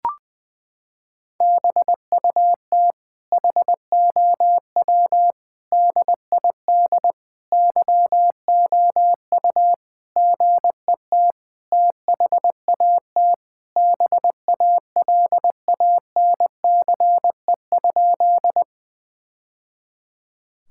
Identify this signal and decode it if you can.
Morse code, heard as a keyed tone, about 20 wpm, 700 Hz. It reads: BUT HOW DID YOU GET THAT BALANCE?
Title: Sentences from Top 1000 Words - Review Part 6 - 20wpm